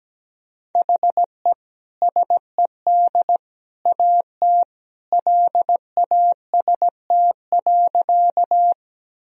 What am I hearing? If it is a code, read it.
Morse code, heard as a keyed tone, about 17 wpm, 700 Hz. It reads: HE SED AT LAST.
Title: Sentences from Top 300 Words and Abbreviations 17wpm